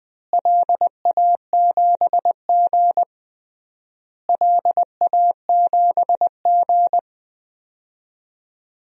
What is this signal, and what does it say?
Morse code, heard as a keyed tone, about 21 wpm, 700 Hz. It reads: LA7G LA7G
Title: DXCC-100 Call Signs - 4 Characters - Repeated 3x - 20wpm